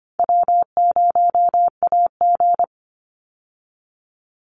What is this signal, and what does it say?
Morse code, heard as a keyed tone, about 25 wpm, 700 Hz. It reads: W0AG
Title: Call Signs Repeated 2x - 25wpm